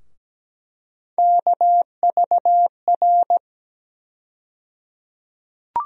Morse code, heard as a keyed tone, about 17 wpm, 700 Hz. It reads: KVR
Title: USA Call Sign Postfixes - 3 Letters 17wpm